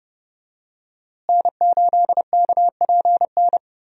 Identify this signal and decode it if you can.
Morse code, heard as a keyed tone, about 30 wpm, 700 Hz. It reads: N8KPN
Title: USA Call Signs - 5 Characters - Repeated 3x - 30wpm